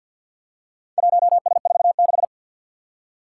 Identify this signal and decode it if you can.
Morse code, heard as a keyed tone, about 45 wpm, 700 Hz. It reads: 1S46
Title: Four Challenging Letter-Number Combinations 50wpm